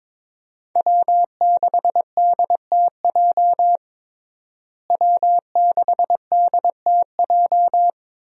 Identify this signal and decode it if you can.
Morse code, heard as a keyed tone, about 22 wpm, 700 Hz. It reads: W6DTJ W6DTJ
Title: USA Call Signs - 5 Characters - Repeated 2x - 22wpm